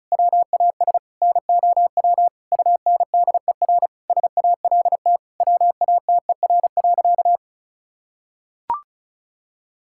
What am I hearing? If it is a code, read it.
Morse code, heard as a keyed tone, about 36 wpm, 700 Hz. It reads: WAS NOW UNDER SALT WATER.
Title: Sentences from Top 1000 Words - Review Part 5 - 35wpm